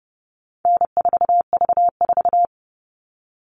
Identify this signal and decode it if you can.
Morse code, heard as a keyed tone, about 30 wpm, 700 Hz. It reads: N4V4
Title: All Cut Numbers - 4-Digit 30wpm